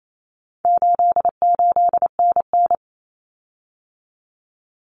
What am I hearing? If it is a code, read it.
Morse code, heard as a keyed tone, about 28 wpm, 700 Hz. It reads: 88NN
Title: Common Cut Numbers - 4-Digit 28wpm